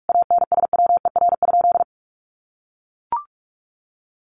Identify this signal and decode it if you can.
Morse code, heard as a keyed tone, about 44 wpm, 700 Hz. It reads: ANSWER?
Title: Sentences from Top 1500 Words 45wpm